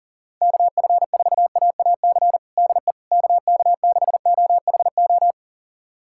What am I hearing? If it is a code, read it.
Morse code, heard as a keyed tone, about 40 wpm, 700 Hz. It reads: KF4AAC DE KK6OHO
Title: USA Callsign DE Callsign 40wpm